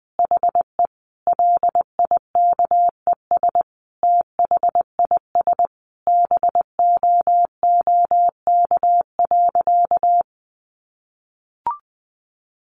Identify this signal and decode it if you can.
Morse code, heard as a keyed tone, about 19 wpm, 700 Hz. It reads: HE LIKES THIS BOOK.
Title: Additional Easy Phrases 20wpm